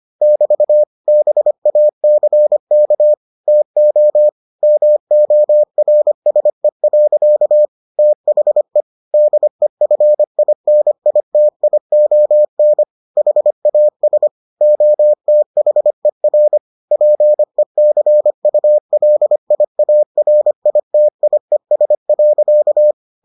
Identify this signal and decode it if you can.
Morse code, heard as a keyed tone, about 25 wpm, 600 Hz. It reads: <BT> BACK TO MORSE. THE DEFINITION HAS OTHER PECULIARITIES.